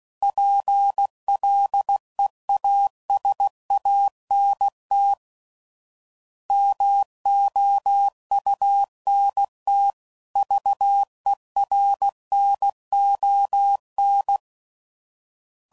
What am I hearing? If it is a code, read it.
Morse code, heard as a keyed tone, about 16 wpm, 800 Hz. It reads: PLEASANT MOUNT VERNON
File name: CW Code Groups @15WPM / fromSM's / Cities